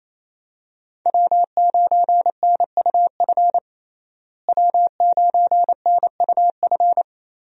Morse code, heard as a keyed tone, about 28 wpm, 700 Hz. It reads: W9NUF W9NUF
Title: USA Call Signs - 5 Characters - Repeated 3x - 28wpm